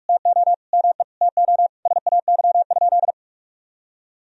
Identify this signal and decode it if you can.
Morse code, heard as a keyed tone, about 46 wpm, 700 Hz. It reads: TO ME TO SAY?
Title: Sentences from Top 200 Words and Abbreviations 45wpm